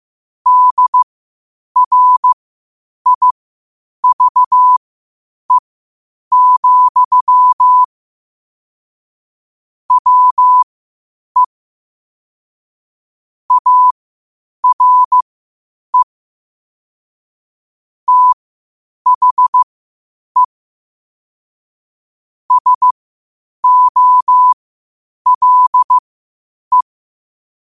Morse code, heard as a keyed tone, about 15 wpm, spaced as for 8 wpm, 1000 Hz. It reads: DRIVE, WE ARE THE SOLE
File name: 8wpm